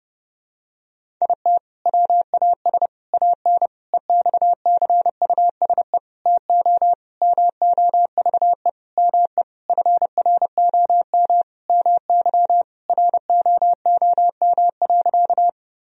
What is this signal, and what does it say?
Morse code, heard as a keyed tone, about 30 wpm, 700 Hz. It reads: IT WAS AN EXCUSE TO MOVE ME FROM MY ROOM.
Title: Sentences from Top 1000 Words - Review Part 1 - 30wpm